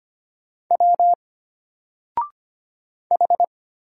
Morse code, heard as a keyed tone, about 25 wpm, 700 Hz. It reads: W H
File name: Single-Letter-Number-25wpm